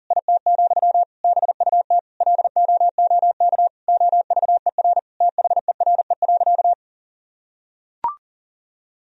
Morse code, heard as a keyed tone, about 40 wpm, 700 Hz. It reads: IT, BUT LOOK OVER THERE.
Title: Sentences from Top 1000 Words - Review Part 2 - 40wpm